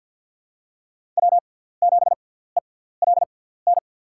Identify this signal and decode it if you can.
Morse code, heard as a keyed tone, about 48 wpm, spaced as for 17 wpm, 700 Hz. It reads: W7ELN
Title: USA Call Signs - 5 Characters - 6x Character Spacing - 50wpm